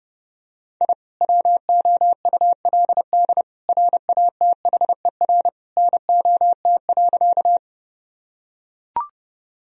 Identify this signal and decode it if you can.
Morse code, heard as a keyed tone, about 30 wpm, 700 Hz. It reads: I WOULD RATHER NOT.
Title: Sentences from Top 1000 Words - Review Part 4 - 30wpm